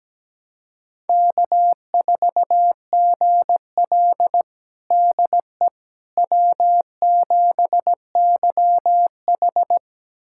Morse code, heard as a keyed tone, about 17 wpm, 700 Hz. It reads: K4GL DE W7YH
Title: USA Callsign DE Callsign - 4 Characters - 17wpm